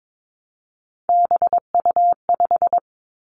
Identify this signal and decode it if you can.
Morse code, heard as a keyed tone, about 22 wpm, 700 Hz. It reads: BU5
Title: Three Challenging Letter-Number Combinations 22wpm